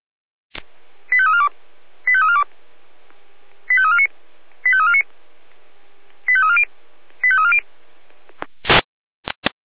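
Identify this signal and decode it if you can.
Five-tone selective call (ZVEI), heard as quick runs of five tones, tones of about 70 ms. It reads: 85342 85342 85349 85349 85340 85340
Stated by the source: ZVEI five-tone selective call 85342 received on SDR.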